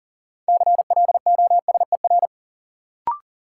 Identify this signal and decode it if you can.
Morse code, heard as a keyed tone, about 40 wpm, 700 Hz. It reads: CLOSER
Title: Sets of 5 Words Encore - New Word Drill - 40wpm